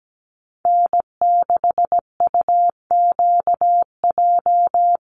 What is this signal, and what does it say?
Morse code, heard as a keyed tone, about 17 wpm, 700 Hz. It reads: N6UQJ
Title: USA Call Signs 17wpm